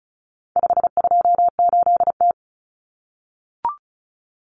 Morse code, heard as a keyed tone, about 34 wpm, 700 Hz. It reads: 528T
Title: Common Cut Numbers - 4-Digit 35wpm